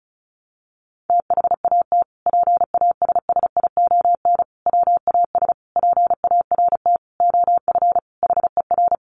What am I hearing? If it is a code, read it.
Morse code, heard as a keyed tone, about 34 wpm, 700 Hz. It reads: THAT PASSION WAS PART OF HER
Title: Sentences from Top 1300 Words 35wpm